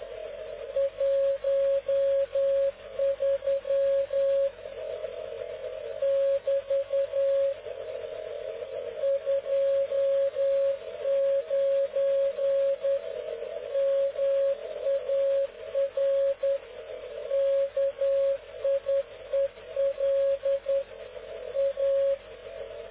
Morse code, heard as a keyed tone, about 10 wpm, 550 Hz. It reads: 13 <BT> 29 MAR KIEL A
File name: cw / ubacw dk0wcy slow